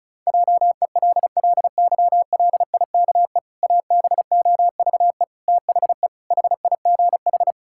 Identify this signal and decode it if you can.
Morse code, heard as a keyed tone, about 36 wpm, 700 Hz. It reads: JELLYLIKE ABOVE THE HIGH
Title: Pangram Challenge v2 - 35wpm